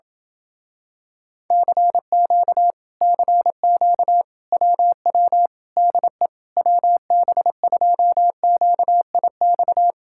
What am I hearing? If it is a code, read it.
Morse code, heard as a keyed tone, about 28 wpm, 700 Hz. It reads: CQ CQ WW DE WB2QIX